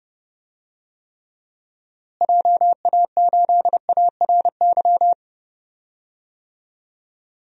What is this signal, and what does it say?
Morse code, heard as a keyed tone, about 30 wpm, 700 Hz. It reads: JA8ARY
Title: DXCC-100 Call Signs - 6 Characters - Repeated 2x - 30wpm